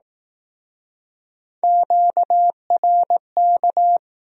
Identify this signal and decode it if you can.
Morse code, heard as a keyed tone, about 20 wpm, 700 Hz. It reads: QRK